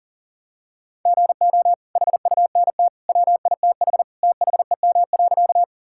Morse code, heard as a keyed tone, about 40 wpm, 700 Hz. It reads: GO HUNT WITH THEM.
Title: Sentences from Top 1600 Words 40wpm